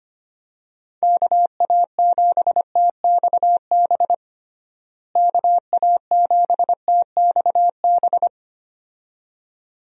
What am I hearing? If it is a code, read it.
Morse code, heard as a keyed tone, about 25 wpm, 700 Hz. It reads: KA7TXB KA7TXB
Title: USA Call Signs - 6 Characters - Repeated 2x - 25wpm